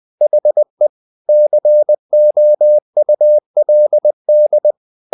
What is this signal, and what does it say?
Morse code, heard as a keyed tone, about 20 wpm, 600 Hz. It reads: HE COULD
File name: the old man and the sea part 2 20wpm